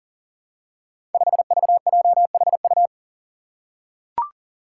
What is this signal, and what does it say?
Morse code, heard as a keyed tone, about 40 wpm, 700 Hz. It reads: 5VJHU
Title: Five Challenging Letter-Number Combinations 40wpm